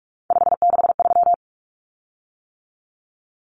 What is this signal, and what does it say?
Morse code, heard as a keyed tone, about 44 wpm, 700 Hz. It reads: 563
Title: Common Cut Numbers - 3-Digit 45wpm